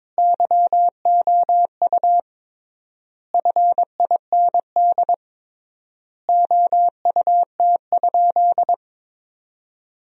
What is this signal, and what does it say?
Morse code, heard as a keyed tone, about 22 wpm, 700 Hz. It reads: YOU FIND OUT?
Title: 3x Word-Spacing - Sentences from Top 200 Words - 22wpm